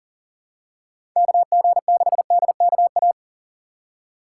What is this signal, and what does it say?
Morse code, heard as a keyed tone, about 37 wpm, 700 Hz. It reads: KG6DKA
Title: USA Call Signs - 4x Character Spacing - 40wpm